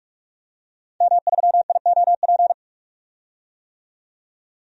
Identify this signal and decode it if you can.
Morse code, heard as a keyed tone, about 47 wpm, 700 Hz. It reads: M3IOP
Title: DXCC-100 Call Signs - 5 Characters - 45wpm